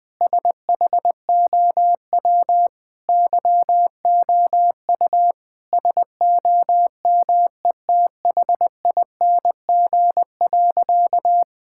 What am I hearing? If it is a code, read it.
Morse code, heard as a keyed tone, about 21 wpm, 700 Hz. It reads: SHOW YOU SOMETHING.
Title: Sentences with 5-Word Phrases - 20wpm